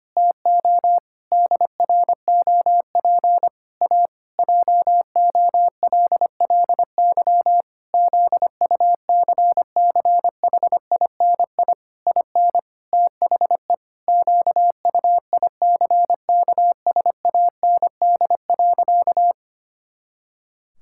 Morse code, heard as a keyed tone, about 25 wpm, 700 Hz. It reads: TO DROP A JOLLY ZUCCHINI IN THE QUICKSAND.
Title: Pangram Challenge v2 - 25wpm